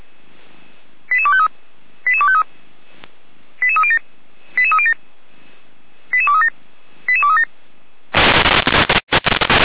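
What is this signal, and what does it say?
Five-tone selective call (ZVEI), heard as quick runs of five tones, tones of about 70 ms. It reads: 88253 88253 88307 88307 88237 88237
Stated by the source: ZVEI five-tone selective call received on an SDR receiver.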